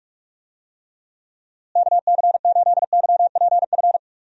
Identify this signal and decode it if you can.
Morse code, heard as a keyed tone, about 45 wpm, 700 Hz. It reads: KC8YPF
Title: USA Call Signs - 6 Characters - Repeated 3x - 45wpm